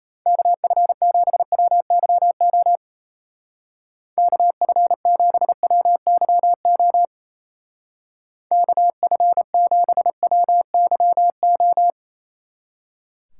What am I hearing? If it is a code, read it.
Morse code, heard as a keyed tone, about 35 wpm, 700 Hz. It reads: KF7WYO KF7WYO KF7WYO
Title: Speed Racing - USA 2x3 Call Signs 28wpm